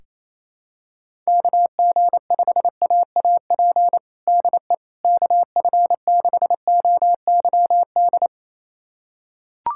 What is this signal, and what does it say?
Morse code, heard as a keyed tone, about 28 wpm, 700 Hz. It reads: KG5AAP DE KF6OYD
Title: USA Callsign DE Callsign - 6 Characters - 28wpm